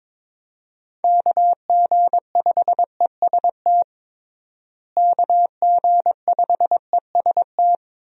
Morse code, heard as a keyed tone, about 22 wpm, 700 Hz. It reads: KG5EST KG5EST
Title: USA Call Signs - 6 Characters - Repeated 3x - 22wpm